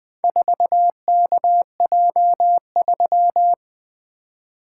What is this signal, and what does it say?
Morse code, heard as a keyed tone, about 20 wpm, 700 Hz. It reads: 4KJ3
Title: Four Letter-Number Combinations - 20wpm